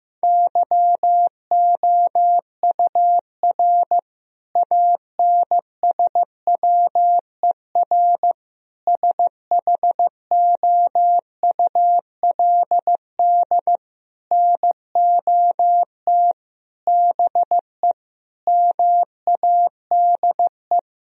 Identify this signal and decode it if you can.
Morse code, heard as a keyed tone, about 15 wpm, 700 Hz. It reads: YOUR ANSWER SHOULD NOT BE MADE
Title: Sentences from Top 1000 Words - Review Part 3 - 15wpm